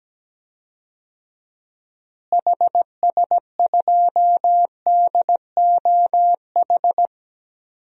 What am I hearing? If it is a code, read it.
Morse code, heard as a keyed tone, about 17 wpm, 700 Hz. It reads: HS2DOH
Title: DXCC-100 Call Signs - 6 Characters - Repeated 2x - 17wpm